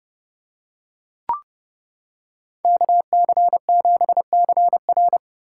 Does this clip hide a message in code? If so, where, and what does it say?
Yes: Morse code from 2.6 to 5.2 s, heard as a keyed tone, about 29 wpm, 700 Hz. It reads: KC7CR